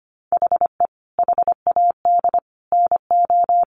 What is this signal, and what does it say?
Morse code, heard as a keyed tone, about 25 wpm, 690 Hz. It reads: HE HAD NO
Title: Sentences from Top 1100 Words 25wpm